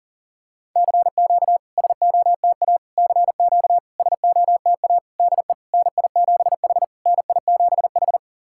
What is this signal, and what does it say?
Morse code, heard as a keyed tone, about 40 wpm, 690 Hz. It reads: CQ SOTA CQ SOTA DE NI7H NI7H
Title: SOTA QSO Practice 40wpm